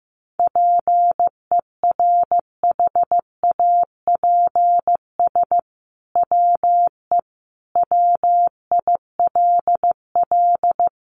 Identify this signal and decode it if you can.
Morse code, heard as a keyed tone, about 15 wpm, 700 Hz. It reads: PERHAPS WE WILL
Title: Sentences from Top 1100 Words 15wpm